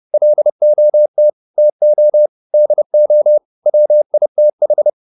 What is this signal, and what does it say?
Morse code, heard as a keyed tone, about 30 wpm, 600 Hz. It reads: LOT TO DO WITH